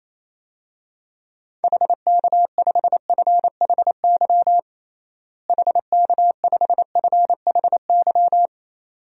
Morse code, heard as a keyed tone, about 28 wpm, 700 Hz. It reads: HK5FHY HK5FHY
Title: DXCC-100 Call Signs - 6 Characters - Repeated 2x - 28wpm